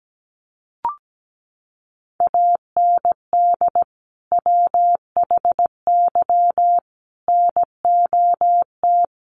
Morse code, heard as a keyed tone, about 17 wpm, 700 Hz. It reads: AND WHY NOT
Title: Sentences from Top 1300 Words 17wpm